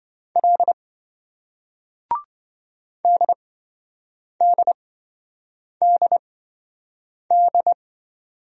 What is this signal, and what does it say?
Morse code, heard as a keyed tone, about 26 wpm, spaced as for 7 wpm, 700 Hz. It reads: L DDDD